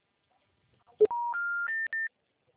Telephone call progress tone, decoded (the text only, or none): special information tone